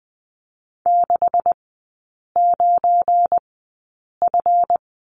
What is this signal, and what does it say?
Morse code, heard as a keyed tone, about 20 wpm, spaced as for 8 wpm, 700 Hz. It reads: 69F
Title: 212 Lesson 34e - Number 8 - Three Characters 20wpm